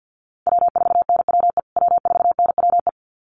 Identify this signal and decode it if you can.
Morse code, heard as a keyed tone, about 50 wpm, 700 Hz. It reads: W4NWE W4NWE
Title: USA Call Signs Repeated 2x 50wpm